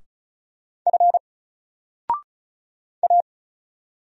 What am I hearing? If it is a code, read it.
Morse code, heard as a keyed tone, about 35 wpm, 700 Hz. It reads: F A
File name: Single-Letter-Number-35wpm